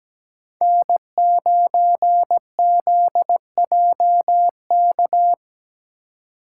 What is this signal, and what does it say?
Morse code, heard as a keyed tone, about 17 wpm, 700 Hz. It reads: N9ZJK
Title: USA Call Signs - 5 Characters - 2x Character Spacing - 17wpm